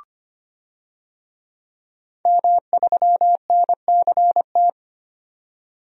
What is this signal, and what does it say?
Morse code, heard as a keyed tone, about 25 wpm, 700 Hz. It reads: M3NCT
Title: DXCC-100 Call Signs - 5 Characters - 25wpm